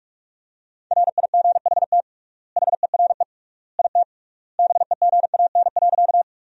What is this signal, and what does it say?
Morse code, heard as a keyed tone, about 42 wpm, 700 Hz. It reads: RIGHT HERE IT BEGAN.
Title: 3x Word-Spacing - Sentences from Top 300 Words - 45wpm